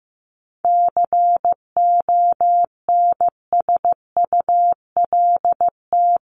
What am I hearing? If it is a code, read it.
Morse code, heard as a keyed tone, about 15 wpm, 700 Hz. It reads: CONSULT
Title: Top 1751-2000 Words 15wpm